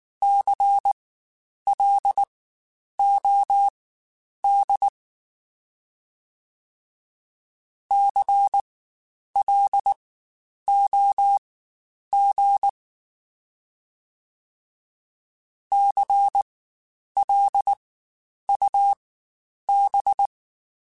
Morse code, heard as a keyed tone, about 19 wpm, spaced as for 9 wpm, 800 Hz. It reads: CLOD CLOG CLUB
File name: Spaced Alphas / Spaced Alphas mp3s / Cs sp @18